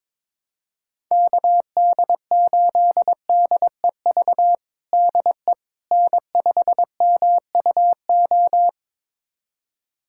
Morse code, heard as a keyed tone, about 22 wpm, 700 Hz. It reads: KD8DEV DE N5MUO
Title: USA Callsign DE Callsign 22wpm